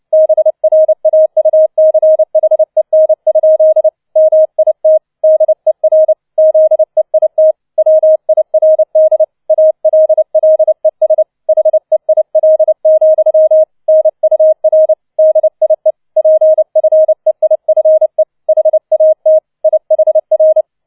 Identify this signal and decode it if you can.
Morse code, heard as a keyed tone, about 29 wpm, 630 Hz. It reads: BRAUCHEN? MIT DER ZEIT WIRD ALLES HEIL, NUR DIE PFEIFE HAT IHR